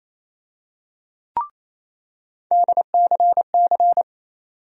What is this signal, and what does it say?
Morse code, heard as a keyed tone, about 28 wpm, 700 Hz. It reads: DCC
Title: USA Call Sign Postfixes - 3 Letters 28wpm